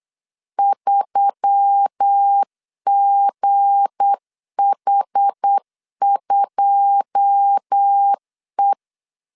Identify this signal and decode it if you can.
Morse code, heard as a keyed tone, about 8 wpm, 800 Hz. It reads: 3GH2E
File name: LEZIONI / 9 2 8wpm 3437